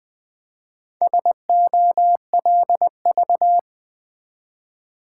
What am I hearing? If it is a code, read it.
Morse code, heard as a keyed tone, about 20 wpm, 700 Hz. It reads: SOLV
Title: Build A Word - 1 Letter At A Time - 5-Letter Words - 20wpm